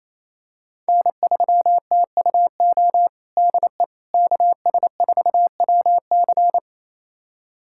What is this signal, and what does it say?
Morse code, heard as a keyed tone, about 27 wpm, 700 Hz. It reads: N3TUO DE KS4WC
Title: USA Callsign DE Callsign - 5 Characters - 28wpm